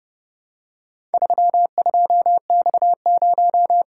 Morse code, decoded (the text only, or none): 32X0